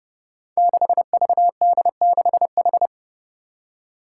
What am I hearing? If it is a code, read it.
Morse code, heard as a keyed tone, about 28 wpm, 700 Hz. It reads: 6VD6H
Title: Five Challenging Letter-Number Combinations 30wpm